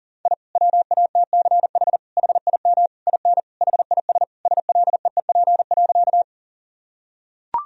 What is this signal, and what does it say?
Morse code, heard as a keyed tone, about 37 wpm, 700 Hz. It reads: I WATCH HIM IN HIS SLEEP.
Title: Sentences from Top 1000 Words - Review Part 1 - 40wpm